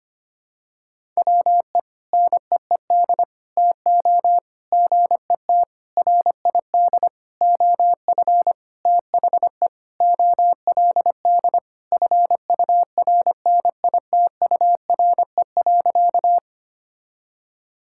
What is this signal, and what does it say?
Morse code, heard as a keyed tone, about 25 wpm, 700 Hz. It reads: WE NEED TO GET RID OF THE OLD FURNITURE.
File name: practice / 25wpm / Sentences with 4-Word Phrases - 25wpm